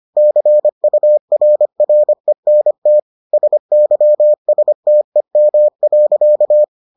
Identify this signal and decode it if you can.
Morse code, heard as a keyed tone, about 25 wpm, 600 Hz. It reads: CURRENT SYSTEM.